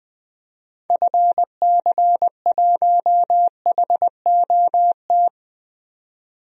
Morse code, heard as a keyed tone, about 21 wpm, 700 Hz. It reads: FC1HOT
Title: DXCC-100 Call Signs - 6 Characters - 20wpm